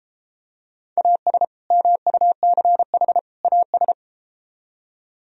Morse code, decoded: AS MUCH AS

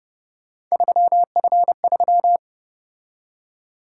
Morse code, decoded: 3F3

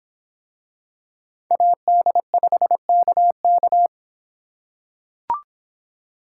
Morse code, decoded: AD5KK

